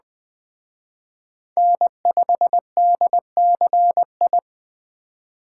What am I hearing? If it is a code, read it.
Morse code, heard as a keyed tone, about 20 wpm, 700 Hz. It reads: N5DCI
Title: Speed Racing - USA 1x3 Call Signs 17wpm